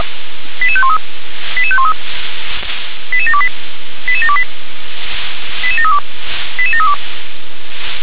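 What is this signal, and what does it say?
Five-tone selective call (ZVEI), heard as quick runs of five tones, tones of about 70 ms. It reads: 88513 88513 88728 88728 88732 88732
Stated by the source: ZVEI five-tone call 88716, received by SDR.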